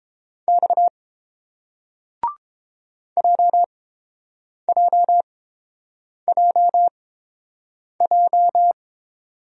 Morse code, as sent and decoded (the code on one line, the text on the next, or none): -..- / .--- .--- .--- .---
X JJJJ